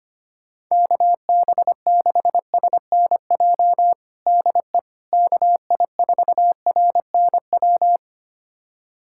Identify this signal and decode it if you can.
Morse code, heard as a keyed tone, about 25 wpm, 700 Hz. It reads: KB6SNJ DE KI4RNW
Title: USA Callsign DE Callsign - 6 Characters - 25wpm